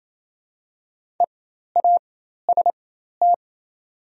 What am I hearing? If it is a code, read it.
Morse code, heard as a keyed tone, about 27 wpm, spaced as for 13 wpm, 700 Hz. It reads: EAST
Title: 4x Character Spacing - Top 500 Words - 28wpm